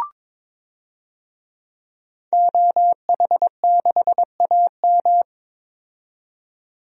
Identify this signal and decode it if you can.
Morse code, heard as a keyed tone, about 21 wpm, 700 Hz. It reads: OH6AM
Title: DXCC-100 Call Signs - 5 Characters - 22wpm